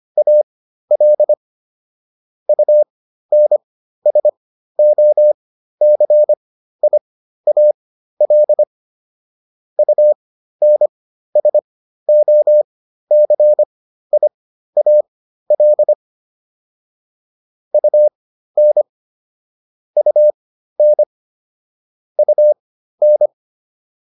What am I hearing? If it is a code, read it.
Morse code, heard as a keyed tone, about 25 wpm, spaced as for 13 wpm, 600 Hz. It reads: AL UNSOCIAL UNSOCIAL UN UN UN